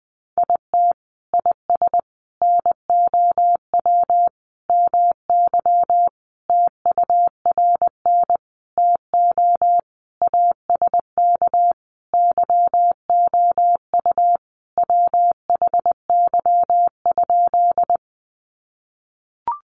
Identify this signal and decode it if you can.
Morse code, heard as a keyed tone, about 20 wpm, 700 Hz. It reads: IT IS NOW MY TURN TO ASK YOU WHY?
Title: Sentences from Top 200 Words 20wpm